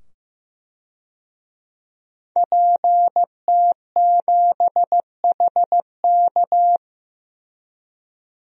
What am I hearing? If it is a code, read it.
Morse code, heard as a keyed tone, about 15 wpm, 700 Hz. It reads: PT7HK
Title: DXCC-100 Call Signs - 5 Characters - 15wpm